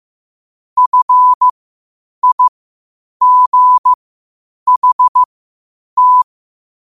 Morse code, heard as a keyed tone, about 15 wpm, spaced as for 8 wpm, 1000 Hz. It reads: FIGHT
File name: qotdfiles / 8wpm